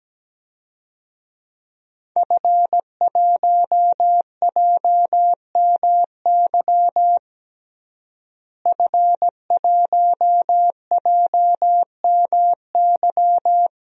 Morse code, heard as a keyed tone, about 17 wpm, 700 Hz. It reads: F1JMY F1JMY
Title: DXCC-100 Call Signs - 5 Characters - Repeated 2x - 17wpm